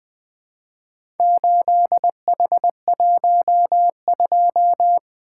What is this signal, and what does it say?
Morse code, heard as a keyed tone, about 20 wpm, 700 Hz. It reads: 8H12
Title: Four Letter-Number Combinations - 20wpm